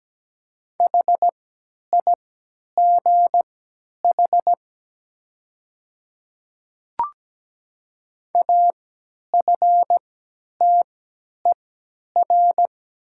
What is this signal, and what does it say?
Morse code, heard as a keyed tone, about 17 wpm, spaced as for 10 wpm, 700 Hz. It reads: HIGH AFTER
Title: 3x Character Spacing - Top 200 Words - 17wpm